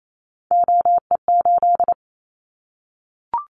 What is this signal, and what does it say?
Morse code, heard as a keyed tone, about 28 wpm, 700 Hz. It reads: OE8
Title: Callsign Prefixes for Top 100 DXCC - 28wpm